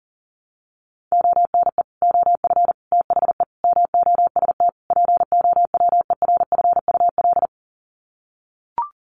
Morse code, heard as a keyed tone, about 40 wpm, 700 Hz. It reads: ONE OF THE MOST POWERFUL
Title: Sets of 5 Words Encore 40wpm